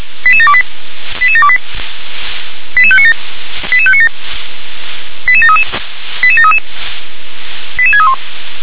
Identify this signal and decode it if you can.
Five-tone selective call (ZVEI), heard as quick runs of five tones, tones of about 70 ms. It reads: 88728 88728 88597 88597 88733 88733 88631
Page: SDR/ZVEI